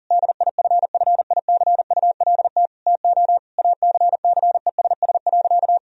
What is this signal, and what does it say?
Morse code, heard as a keyed tone, about 40 wpm, 700 Hz. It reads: DIFFICULT TO ACCESS.
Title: Sentences from Top 1000 Words - Review Part 4 - 40wpm